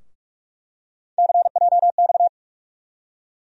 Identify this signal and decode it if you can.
Morse code, heard as a keyed tone, about 46 wpm, 700 Hz. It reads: CJX
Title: USA Call Sign Postfixes - 3 Letters 45wpm